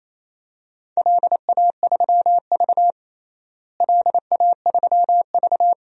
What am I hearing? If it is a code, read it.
Morse code, heard as a keyed tone, about 28 wpm, 700 Hz. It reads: LA3V LA3V
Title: DXCC-100 Call Signs - 4 Characters - Repeated 2x - 28wpm